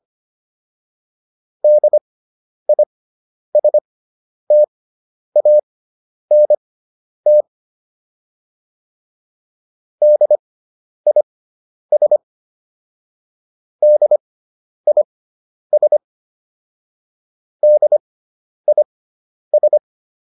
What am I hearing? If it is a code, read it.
Morse code, heard as a keyed tone, about 25 wpm, spaced as for 10 wpm, 600 Hz. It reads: DISTANT DIS DIS DIS